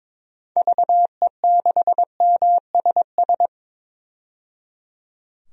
Morse code, heard as a keyed tone, about 22 wpm, 700 Hz. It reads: VE6MSS
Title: DXCC-100 Call Signs - 6 Characters - 22wpm